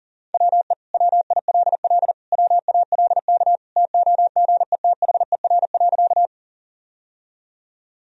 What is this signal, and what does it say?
Morse code, heard as a keyed tone, about 40 wpm, 700 Hz. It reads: WE WILL WALK TOGETHER.
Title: Sentences from Top 1000 Words - Review Part 1 - 40wpm